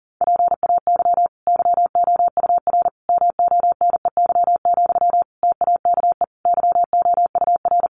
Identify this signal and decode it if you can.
Morse code, heard as a keyed tone, about 40 wpm, 700 Hz. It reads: PAY YOUR MONEY, TAKE YOUR
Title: Sentences from Top 900 Words 40wpm